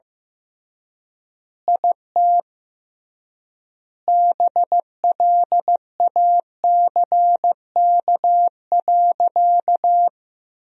Morse code, heard as a keyed tone, about 15 wpm, 700 Hz. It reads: IT BLACK.
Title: 3x Word-Spacing - Sentences from Top 400 Words - 15wpm